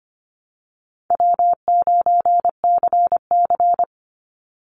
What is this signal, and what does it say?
Morse code, heard as a keyed tone, about 25 wpm, 700 Hz. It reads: W9CC
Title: Easy Call Signs Repeated 2x - 25wpm